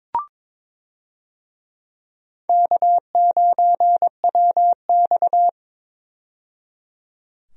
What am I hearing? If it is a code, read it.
Morse code, heard as a keyed tone, about 22 wpm, 700 Hz. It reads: K9WX
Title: USA Call Signs - 4 Characters - 22wpm